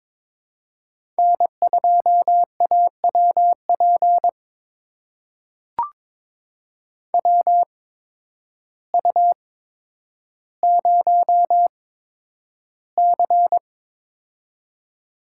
Morse code, heard as a keyed tone, about 22 wpm, spaced as for 10 wpm, 700 Hz. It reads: N2AWP WU0C
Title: USA Call Signs - 8x Character Spacing - 22wpm